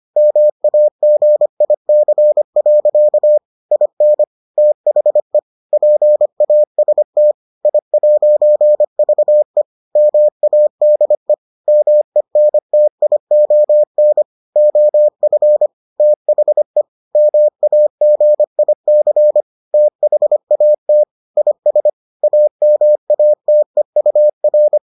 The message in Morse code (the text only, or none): MAGIC. IN THE PAST I'VE MADE MENTION OF THE MAGIC THAT IS AMATEUR